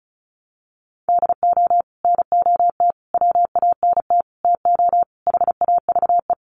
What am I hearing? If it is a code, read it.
Morse code, heard as a keyed tone, about 34 wpm, 700 Hz. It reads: DO NOT WANT TO HAVE
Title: Sets of 5 Words 35wpm